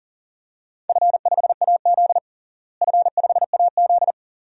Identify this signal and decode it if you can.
Morse code, heard as a keyed tone, about 40 wpm, 700 Hz. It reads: F5AZ F5AZ
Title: DXCC-100 Call Signs - 4 Characters - Repeated 2x - 40wpm